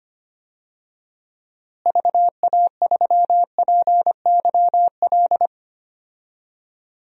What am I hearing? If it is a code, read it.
Morse code, heard as a keyed tone, about 25 wpm, 700 Hz. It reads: VA3PYL
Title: DXCC-100 Call Signs - 6 Characters - Repeated 3x - 25wpm